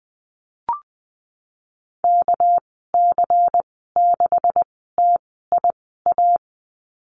Morse code, heard as a keyed tone, about 19 wpm, spaced as for 14 wpm, 700 Hz. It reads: KC6TIA
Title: 231 Lesson 37g - Letter x - Call Signs 20wpm